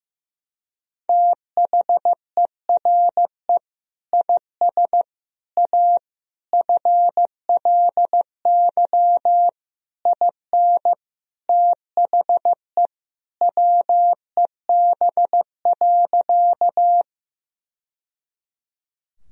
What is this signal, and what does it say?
Morse code, heard as a keyed tone, about 15 wpm, 700 Hz. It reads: THERE IS A FLY IN THE WEB.
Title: Sentences from Top 1000 Words - Review Part 4 - 15wpm